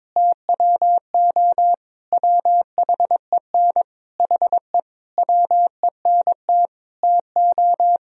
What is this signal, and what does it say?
Morse code, heard as a keyed tone, about 22 wpm, 700 Hz. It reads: TWO WHEN HE WENT TO
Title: Sentences from Top 1000 Words - Review Part 3 - 22wpm